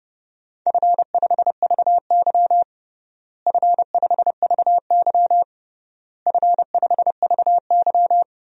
Morse code, heard as a keyed tone, about 30 wpm, 700 Hz. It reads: F5VY F5VY F5VY
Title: DXCC-100 Call Signs - 4 Characters - Repeated 3x - 30wpm